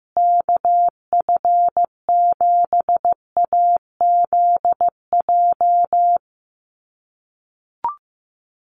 Morse code, heard as a keyed tone, about 15 wpm, 700 Hz. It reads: KF7AZJ
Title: USA Call Signs Repeated 3x 15wpm